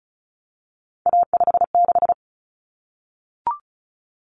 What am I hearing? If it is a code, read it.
Morse code, heard as a keyed tone, about 34 wpm, 700 Hz. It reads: A56
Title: Common Cut Numbers - 3-Digit 35wpm